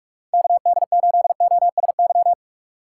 Morse code, heard as a keyed tone, about 46 wpm, 690 Hz. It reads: KD8OSY
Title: USA Call Signs - 6 Characters - 2x Character Spacing - 45wpm